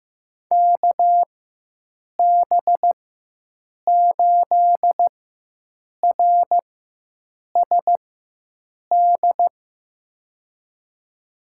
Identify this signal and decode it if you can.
Morse code, heard as a keyed tone, about 15 wpm, spaced as for 7 wpm, 700 Hz. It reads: KB8RSD
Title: USA Call Signs - 6 Characters - 4x Character Spacing - 15wpm